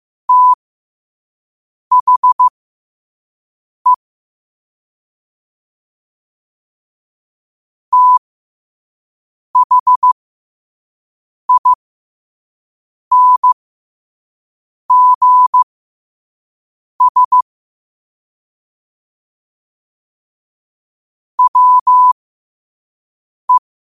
Morse code, heard as a keyed tone, about 15 wpm, spaced as for 5 wpm, 1000 Hz. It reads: THE THINGS WE
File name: qotdfiles / 5wpm